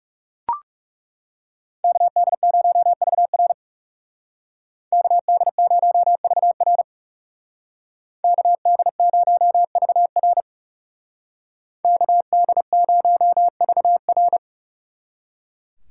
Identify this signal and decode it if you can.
Morse code, heard as a keyed tone, about 37 wpm, 700 Hz. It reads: KD0VR KD0VR KD0VR KD0VR
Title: Speed Racing - USA 2x2 Call Signs 30wpm